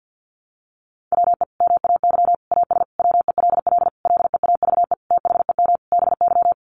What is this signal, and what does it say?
Morse code, heard as a keyed tone, about 50 wpm, 700 Hz. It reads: WE MAY AS WELL LEAVE THEM BY
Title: Sentences from Top 800 Words 50wpm